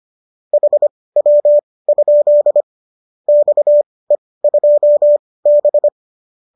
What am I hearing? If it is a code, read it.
Morse code, heard as a keyed tone, about 25 wpm, spaced as for 18 wpm, 590 Hz. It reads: HW? XE2B
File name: qso204 18